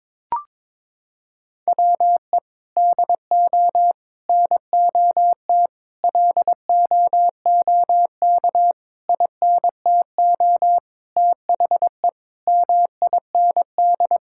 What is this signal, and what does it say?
Morse code, heard as a keyed tone, about 22 wpm, 700 Hz. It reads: WE DO NOT LOOK INTO THE MIND
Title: Sentences from Top 1600 Words 22wpm